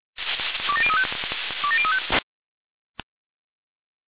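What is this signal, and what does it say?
Five-tone selective call (ZVEI), heard as quick runs of five tones, tones of about 70 ms. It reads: 27035 27035
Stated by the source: Five-tone ZVEI selective call received by SDR.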